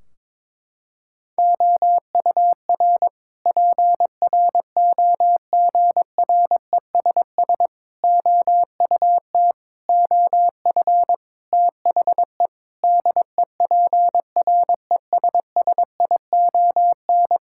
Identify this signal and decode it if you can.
Morse code, heard as a keyed tone, about 22 wpm, 700 Hz. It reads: OUR PROGRESS OUT OF THE DEPRESSION
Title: Sentences from Top 1600 Words 22wpm